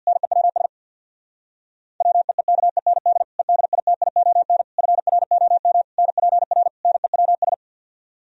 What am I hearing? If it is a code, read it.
Morse code, heard as a keyed tone, about 64 wpm, 700 Hz. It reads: NEWS WEEKEND EDITION FROM NPR NEWS